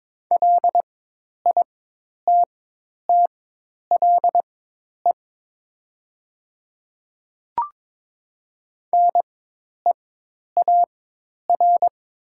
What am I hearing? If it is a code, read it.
Morse code, heard as a keyed tone, about 22 wpm, spaced as for 10 wpm, 700 Hz. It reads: LITTLE NEAR
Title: 4x Character Spacing - Top 200 Words - 22wpm